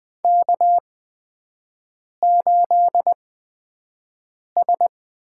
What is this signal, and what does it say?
Morse code, heard as a keyed tone, about 20 wpm, spaced as for 5 wpm, 700 Hz. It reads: K8S
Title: USA Call Signs - 8x Character Spacing - 20wpm